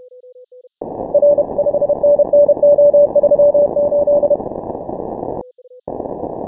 Morse code, heard as a keyed tone, about 31 wpm, 580 Hz. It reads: R 5NN O38
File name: final / W1UJ-168-140-MorseRunner